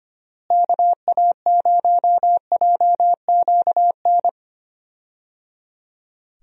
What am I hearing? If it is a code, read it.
Morse code, heard as a keyed tone, about 25 wpm, 700 Hz. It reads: KA0JQN
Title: USA Call Signs - 6 Characters - Repeated 3x - 25wpm